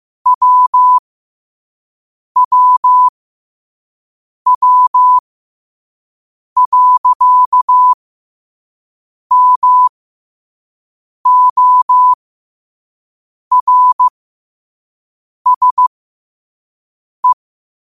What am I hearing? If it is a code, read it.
Morse code, heard as a keyed tone, about 15 wpm, spaced as for 5 wpm, 1000 Hz. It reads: WWW.MORSE
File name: qotdfiles / 5wpm